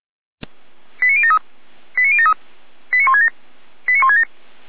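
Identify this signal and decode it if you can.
Five-tone selective call (ZVEI), heard as quick runs of five tones, tones of about 70 ms. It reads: 89073 89073 89167 89167
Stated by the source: ZVEI five-tone selective call received by SDR.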